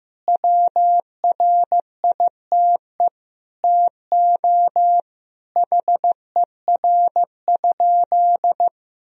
Morse code, heard as a keyed tone, about 15 wpm, 700 Hz. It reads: WRITE TO HER?
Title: Sentences from Top 1000 Words - Review Part 1 - 15wpm